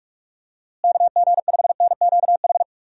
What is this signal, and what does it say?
Morse code, heard as a keyed tone, about 41 wpm, 700 Hz. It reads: KG5NQH
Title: USA Call Signs - 6 Characters - Repeated 2x - 45wpm